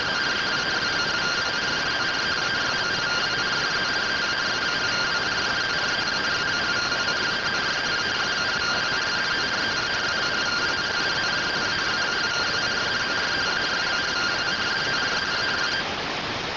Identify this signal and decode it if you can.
Two-tone teleprinter signal, about 170 Hz shift.